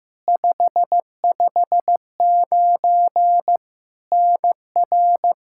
Morse code, heard as a keyed tone, about 15 wpm, 700 Hz. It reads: 559 NR
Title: Practice QSO - Exchange 1 - Beginning - 15wpm